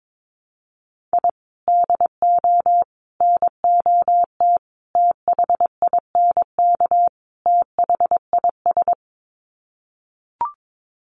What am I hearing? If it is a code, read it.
Morse code, heard as a keyed tone, about 22 wpm, 700 Hz. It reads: I DO NOT THINK THIS
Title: Sets of 5 Words Encore 22wpm